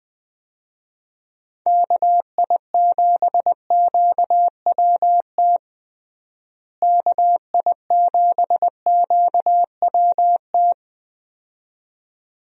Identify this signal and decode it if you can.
Morse code, heard as a keyed tone, about 21 wpm, 700 Hz. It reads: KI7QWT KI7QWT
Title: USA Call Signs - 6 Characters - Repeated 2x - 20wpm